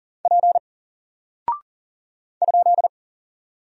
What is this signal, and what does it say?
Morse code, heard as a keyed tone, about 39 wpm, 700 Hz. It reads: P ?
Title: Single Character (Letter:Number:Punctuation:Prosign) Alt 40wpm